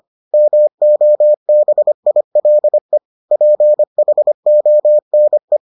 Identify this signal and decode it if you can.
Morse code, heard as a keyed tone, about 25 wpm, 600 Hz. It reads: MOBILE PHONE